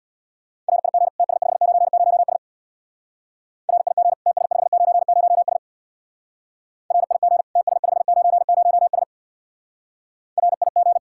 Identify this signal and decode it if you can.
Morse code, heard as a keyed tone, about 65 wpm, 700 Hz. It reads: RIG TS590S RIG TS590S RIG TS590S RIG